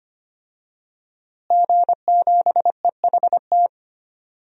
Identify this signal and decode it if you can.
Morse code, heard as a keyed tone, about 25 wpm, 700 Hz. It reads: G7EHT
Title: DXCC-100 Call Signs - 5 Characters - Repeated 2x - 25wpm